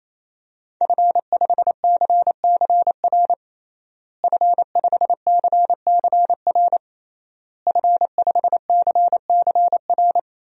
Morse code, heard as a keyed tone, about 28 wpm, 700 Hz. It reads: F5CCR F5CCR F5CCR
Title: DXCC-100 Call Signs - 5 Characters - Repeated 3x - 28wpm